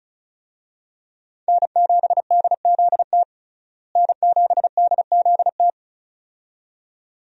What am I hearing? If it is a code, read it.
Morse code, heard as a keyed tone, about 34 wpm, 700 Hz. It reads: N7DZT N7DZT
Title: USA Call Signs - 5 Characters - Repeated 2x - 35wpm